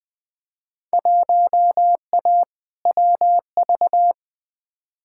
Morse code, heard as a keyed tone, about 20 wpm, 700 Hz. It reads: 1A WV
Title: Field Day Exchanges Encore - 20wpm